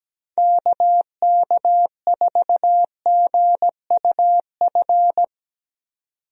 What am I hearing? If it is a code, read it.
Morse code, heard as a keyed tone, about 17 wpm, 700 Hz. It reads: KK4GUF
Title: USA Call Signs - 6 Characters - 17wpm